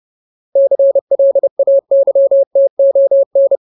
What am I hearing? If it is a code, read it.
Morse code, heard as a keyed tone, about 30 wpm, 550 Hz. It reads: CLAYTON
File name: Ultimate Word List / 30WPM Ultimate